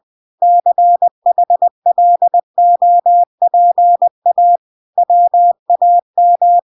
Morse code, heard as a keyed tone, about 20 wpm, 700 Hz. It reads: CHLOPA WAM